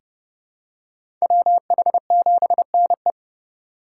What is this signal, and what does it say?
Morse code, heard as a keyed tone, about 30 wpm, 700 Hz. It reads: WH7NE
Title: USA Call Signs - 4x Character Spacing - 30wpm